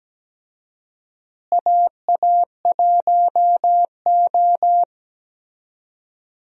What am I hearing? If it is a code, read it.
Morse code, heard as a keyed tone, about 17 wpm, 700 Hz. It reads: AA1O
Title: USA Call Signs - 4 Characters - Repeated 2x - 17wpm